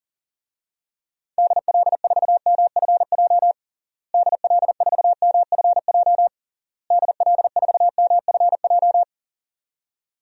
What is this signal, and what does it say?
Morse code, heard as a keyed tone, about 37 wpm, 700 Hz. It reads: DL4MFJ DL4MFJ DL4MFJ
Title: DXCC-100 Call Signs - 6 Characters - Repeated 3x - 40wpm